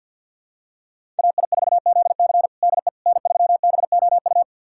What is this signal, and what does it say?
Morse code, heard as a keyed tone, about 46 wpm, 700 Hz. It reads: AI4ZC DE N3BOU
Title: USA Callsign DE Callsign - 5 Characters - 50wpm